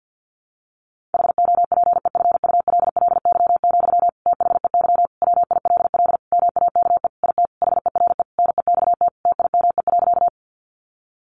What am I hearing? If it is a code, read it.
Morse code, heard as a keyed tone, about 50 wpm, 700 Hz. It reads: HOPEFULLY, THEY WILL MAKE IT HERE NEXT TIME.
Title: Sentences from Top 1500 Words 50wpm